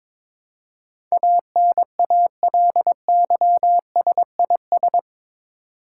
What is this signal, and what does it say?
Morse code, heard as a keyed tone, about 22 wpm, 700 Hz. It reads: ANALYSIS
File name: practice / 22wpm / Eight Letter Words - 22wpm